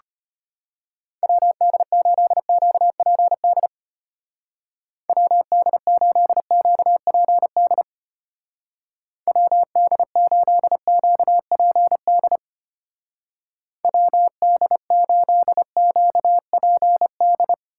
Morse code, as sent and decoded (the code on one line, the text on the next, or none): .-- -.. ---.. --.- .--. -.. / .-- -.. ---.. --.- .--. -.. / .-- -.. ---.. --.- .--. -.. / .-- -.. ---.. --.- .--. -..
WD8QPD WD8QPD WD8QPD WD8QPD